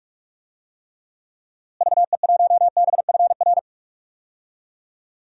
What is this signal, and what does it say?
Morse code, heard as a keyed tone, about 44 wpm, 700 Hz. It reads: VE1BFR
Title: DXCC-100 Call Signs - 6 Characters - Repeated 2x - 45wpm